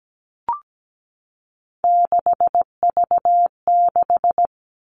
Morse code, heard as a keyed tone, about 17 wpm, 700 Hz. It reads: 6V6
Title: Three Challenging Letter-Number Combinations 17wpm